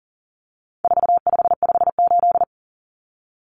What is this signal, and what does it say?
Morse code, heard as a keyed tone, about 40 wpm, 700 Hz. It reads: 4558
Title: Numbers - 4-Digit 40wpm